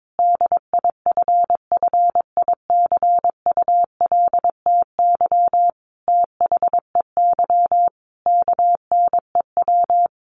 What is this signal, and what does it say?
Morse code, heard as a keyed tone, about 22 wpm, 700 Hz. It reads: DIFFICULTY THEY KNEW
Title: Sentences from Top 1500 Words 22wpm